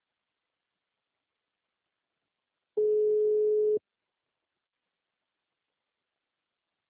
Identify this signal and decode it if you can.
Ringback tone (Europe, 425 Hz).
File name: out- -2023- - - .47809